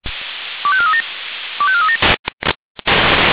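Five-tone selective call (ZVEI), heard as quick runs of five tones, tones of about 70 ms. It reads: 25437 25437
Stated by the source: ZVEI five-tone selective call received on SDR.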